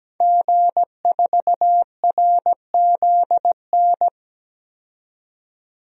Morse code, heard as a keyed tone, about 17 wpm, 700 Hz. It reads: G4RZN
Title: DXCC-100 Call Signs - 5 Characters - 17wpm